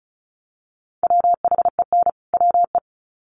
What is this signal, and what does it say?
Morse code, heard as a keyed tone, about 34 wpm, 700 Hz. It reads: WHEN WE
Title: Sets of 2 Words 35wpm